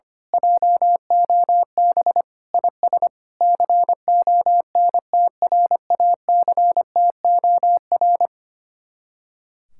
Morse code, heard as a keyed tone, about 24 wpm, 700 Hz. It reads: JOB IS CONTRACTOR
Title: Speed-Racing Common QSO Phrases Encore - 25wpm